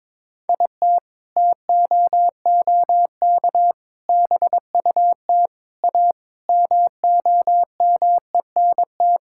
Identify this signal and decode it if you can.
Morse code, heard as a keyed tone, about 22 wpm, 700 Hz. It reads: IT TOOK BUT A MOMENT
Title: Sentences from Top 1600 Words 22wpm